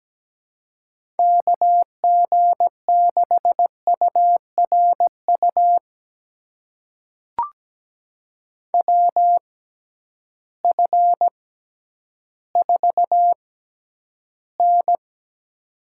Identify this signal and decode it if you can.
Morse code, heard as a keyed tone, about 17 wpm, spaced as for 10 wpm, 700 Hz. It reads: KG6URU WF4N
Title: USA Call Signs - 6x Character Spacing - 17wpm